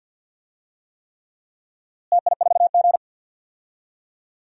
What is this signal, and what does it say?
Morse code, heard as a keyed tone, about 51 wpm, 700 Hz. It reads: TI4G
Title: DXCC-100 Call Signs - 4 Characters - Repeated 3x - 50wpm